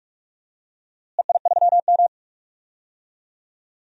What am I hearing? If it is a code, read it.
Morse code, heard as a keyed tone, about 42 wpm, 700 Hz. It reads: EI3M
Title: DXCC-100 Call Signs - 4 Characters - Repeated 3x - 45wpm